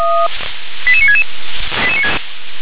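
Five-tone selective call (ZVEI), heard as quick runs of five tones, tones of about 80 ms. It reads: 88966
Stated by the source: ZVEI five-tone selective call received on SDR.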